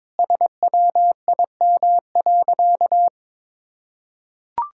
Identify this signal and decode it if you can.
Morse code, heard as a keyed tone, about 22 wpm, 700 Hz. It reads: SWIM.
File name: practice / 22wpm / Sentences with 3-Word Phrases - 22wpm